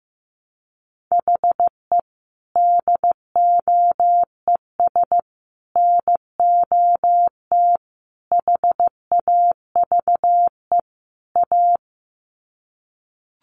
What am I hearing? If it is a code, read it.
Morse code, heard as a keyed tone, about 15 wpm, 700 Hz. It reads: HE DOES NOT HAVE A
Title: Sets of 5 Words Encore 15wpm